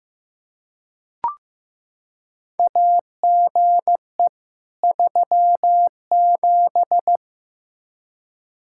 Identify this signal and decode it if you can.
Morse code, heard as a keyed tone, about 15 wpm, 700 Hz. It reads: AGE 37
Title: K1USN SST QSO Practice 15wpm